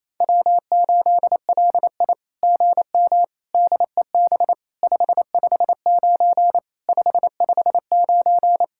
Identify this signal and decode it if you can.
Morse code, heard as a keyed tone, about 27 wpm, 700 Hz. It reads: W8LI GM DEB 559 559
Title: SOTA QSO Practice 28wpm